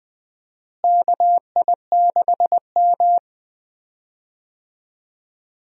Morse code, heard as a keyed tone, about 20 wpm, 700 Hz. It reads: KI6M
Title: USA Call Signs - 4 Characters - Repeated 2x - 20wpm